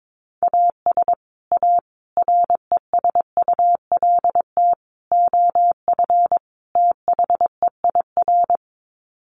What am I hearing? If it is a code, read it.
Morse code, heard as a keyed tone, about 22 wpm, 700 Hz. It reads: AS A RESULT OF THEIR
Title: Sets of 5 Words Encore2 22wpm